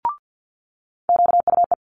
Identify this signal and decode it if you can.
Morse code, heard as a keyed tone, about 50 wpm, 700 Hz. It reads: QUE
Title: Multi-letter Phonemes Quick Fire 50wpm